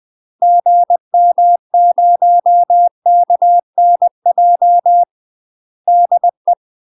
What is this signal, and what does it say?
Morse code, heard as a keyed tone, about 20 wpm, 700 Hz. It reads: GM0KNJ DE
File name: UK MASTER QSOs TRAINING/20WPM